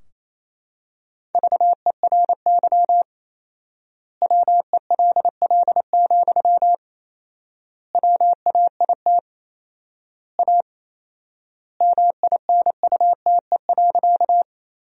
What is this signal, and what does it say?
Morse code, heard as a keyed tone, about 28 wpm, 700 Hz. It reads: VERY WELL, WAIT A MINUTE.
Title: 4x Word-Spacing - Sentences from Top 500 Words - 28wpm